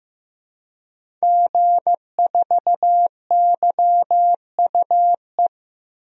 Morse code, heard as a keyed tone, about 15 wpm, 700 Hz. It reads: G4YUE
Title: DXCC-100 Call Signs - 5 Characters - Repeated 2x - 15wpm